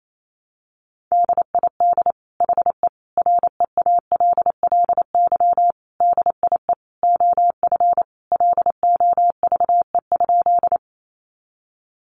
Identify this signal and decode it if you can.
Morse code, heard as a keyed tone, about 28 wpm, 700 Hz. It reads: DID HE REALLY DIE OF LOVE?
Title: Sentences from Top 1300 Words 28wpm